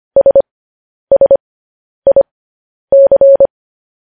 Morse code, heard as a keyed tone, about 25 wpm, spaced as for 10 wpm, 550 Hz. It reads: SSIC
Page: news headlines in morse code at 10 wpm